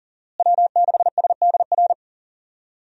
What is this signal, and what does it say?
Morse code, heard as a keyed tone, about 40 wpm, 700 Hz. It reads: W6SDR
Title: USA Call Signs - 5 Characters - Repeated 2x - 40wpm